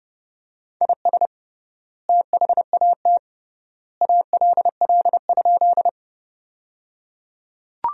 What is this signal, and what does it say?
Morse code, heard as a keyed tone, about 30 wpm, 700 Hz. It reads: IS THAT ALL?
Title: 3x Word-Spacing - Sentences from Top 100 Words - 30wpm